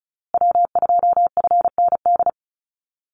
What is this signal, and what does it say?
Morse code, heard as a keyed tone, about 34 wpm, 700 Hz. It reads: W2FND
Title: USA Call Signs Repeated 2x 35wpm